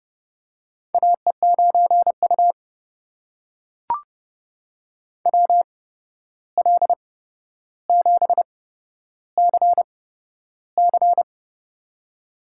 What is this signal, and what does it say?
Morse code, heard as a keyed tone, about 30 wpm, spaced as for 12 wpm, 700 Hz. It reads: AE9U WL7CC
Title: USA Call Signs - 8x Character Spacing - 30wpm